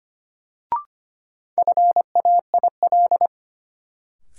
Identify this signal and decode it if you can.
Morse code, heard as a keyed tone, about 25 wpm, 700 Hz. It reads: FAIL
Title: Sentences from Top 1100 Words - New Word Drill - 25wpm